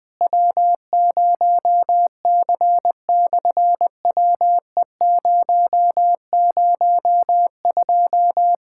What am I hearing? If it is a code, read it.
Morse code, heard as a keyed tone, about 20 wpm, 700 Hz. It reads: W0C/WE002
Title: SOTA QSO Practice 20wpm